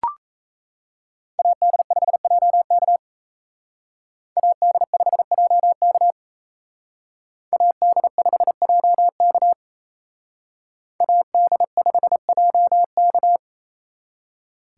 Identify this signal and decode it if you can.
Morse code, heard as a keyed tone, about 34 wpm, 700 Hz. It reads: AD5JK AD5JK AD5JK AD5JK
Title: Speed Racing - USA 2x2 Call Signs 28wpm